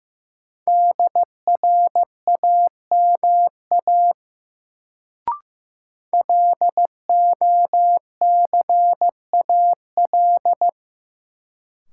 Morse code, heard as a keyed tone, about 15 wpm, 700 Hz. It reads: DRAMA LOCAL
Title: Sentences from Top 1500 Words - New Word Drill Alt - 15wpm